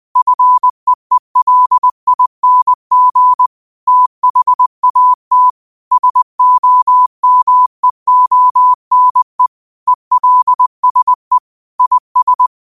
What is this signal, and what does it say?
Morse code, heard as a keyed tone, about 20 wpm, 1000 Hz. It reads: FEELING THAT SOMEONE ELSE IS